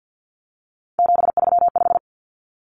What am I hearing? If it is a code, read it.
Morse code, heard as a keyed tone, about 50 wpm, 700 Hz. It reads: 735
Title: Common Cut Numbers - 3-Digit 50wpm